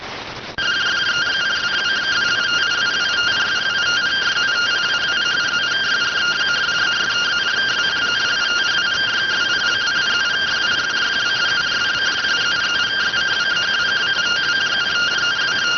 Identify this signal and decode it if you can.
Two-tone teleprinter signal, about 170 Hz shift.